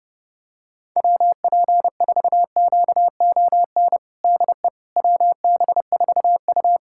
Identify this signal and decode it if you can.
Morse code, heard as a keyed tone, about 30 wpm, 700 Hz. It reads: WP4QON DE WB4U
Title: USA Callsign DE Callsign 30wpm